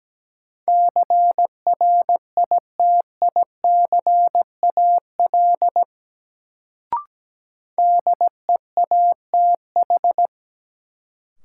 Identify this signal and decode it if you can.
Morse code, heard as a keyed tone, about 17 wpm, 700 Hz. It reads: CRITICAL DEATH
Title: Sentences from Top 1100 Words - New Word Drill - 17wpm